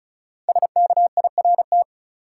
Morse code, heard as a keyed tone, about 36 wpm, 700 Hz. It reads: SKIRT
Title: Five Letter Words - 35wpm